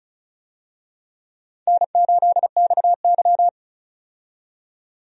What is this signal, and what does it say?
Morse code, heard as a keyed tone, about 34 wpm, 700 Hz. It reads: N8XY
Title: USA Call Signs - 4 Characters - 35wpm